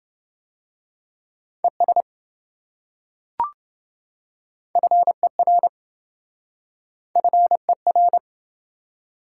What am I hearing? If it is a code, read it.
Morse code, heard as a keyed tone, about 26 wpm, 700 Hz. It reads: ES FER FER